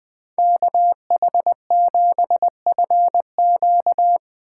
Morse code, heard as a keyed tone, about 20 wpm, 700 Hz. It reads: KH7FQ